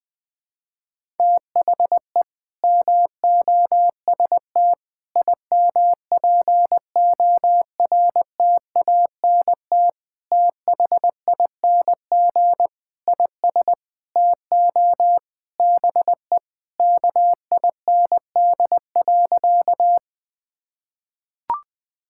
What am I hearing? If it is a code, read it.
Morse code, heard as a keyed tone, about 20 wpm, 700 Hz. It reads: THE MOST IMPORTANT THING IS TO BE KIND.
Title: Sentences with 4-Word Phrases - 20wpm